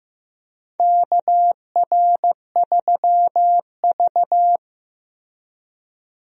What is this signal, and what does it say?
Morse code, heard as a keyed tone, about 15 wpm, 700 Hz. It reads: KR3V
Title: USA Call Signs - 4 Characters - Repeated 2x - 15wpm